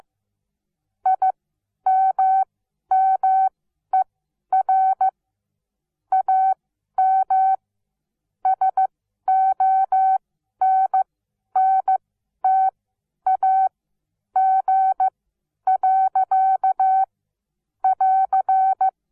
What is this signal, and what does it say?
Morse code, heard as a keyed tone, about 15 wpm, spaced as for 10 wpm, 760 Hz. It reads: IMMER AM SONNTAG. <AR>
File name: Onlinelehrgang / morsen / DJ4UF Morsen17